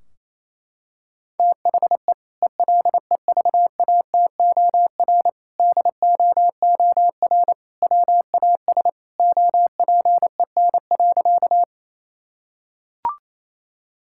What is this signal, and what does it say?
Morse code, heard as a keyed tone, about 28 wpm, 700 Hz. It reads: THE ELEVATOR DOOR WAS OPEN.
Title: Sentences from Top 1600 Words 28wpm